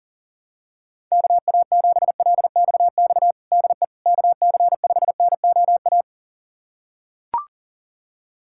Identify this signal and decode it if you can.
Morse code, heard as a keyed tone, about 40 wpm, 700 Hz. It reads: KA7LXX DE KC5NOA
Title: USA Callsign DE Callsign 40wpm